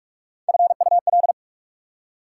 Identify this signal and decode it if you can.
Morse code, heard as a keyed tone, about 60 wpm, spaced as for 52 wpm, 700 Hz. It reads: FUL
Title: Speed-Racing Prefixes and Suffixes 45wpm